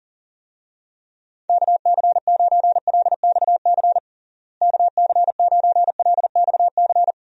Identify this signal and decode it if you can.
Morse code, heard as a keyed tone, about 40 wpm, 700 Hz. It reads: KC9LXC KC9LXC
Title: USA Call Signs - 6 Characters - Repeated 2x - 40wpm